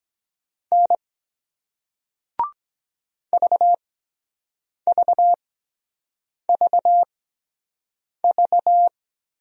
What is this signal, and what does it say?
Morse code, heard as a keyed tone, about 22 wpm, spaced as for 7 wpm, 700 Hz. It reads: N VVVV